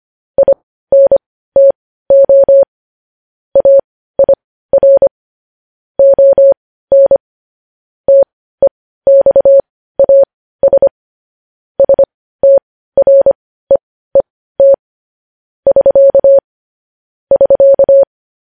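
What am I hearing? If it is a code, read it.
Morse code, heard as a keyed tone, about 25 wpm, spaced as for 15 wpm, 550 Hz. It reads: INTO AIR ON TEXAS STREET <SK> <SK>
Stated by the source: Morse code news headlines sent at 15.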